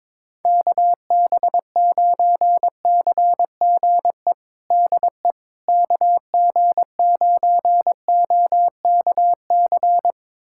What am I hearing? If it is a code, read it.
Morse code, heard as a keyed tone, about 21 wpm, 700 Hz. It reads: KB9CGE DE KG9OKC
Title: USA Callsign DE Callsign - 6 Characters - 22wpm